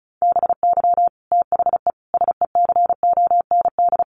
Morse code, heard as a keyed tone, about 34 wpm, 700 Hz. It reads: BY THE SECOND
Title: Sentences from Top 1200 Words 35wpm